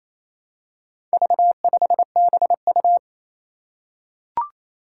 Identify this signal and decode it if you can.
Morse code, heard as a keyed tone, about 28 wpm, 700 Hz. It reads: V5BU
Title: Four Challenging Letter-Number Combinations 28wpm